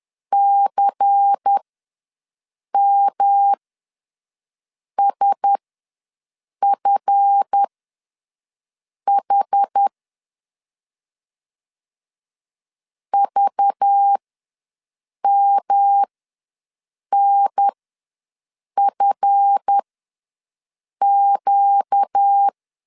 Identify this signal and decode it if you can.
Morse code, heard as a keyed tone, about 10 wpm, spaced as for 5 wpm, 800 Hz. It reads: CMSFH VMNFQ
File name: LEZIONI / 7 GF9Z / 7 GF ws 10wpm 3918